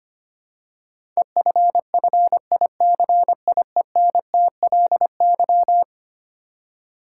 Morse code, heard as a keyed tone, about 25 wpm, 700 Hz. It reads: EFFICIENTLY
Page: morse-code-ninja-practice-25wpm